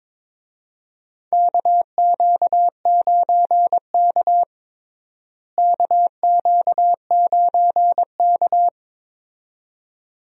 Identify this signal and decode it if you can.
Morse code, heard as a keyed tone, about 22 wpm, 700 Hz. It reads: KQ9K KQ9K
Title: USA Call Signs - 4 Characters - Repeated 2x - 22wpm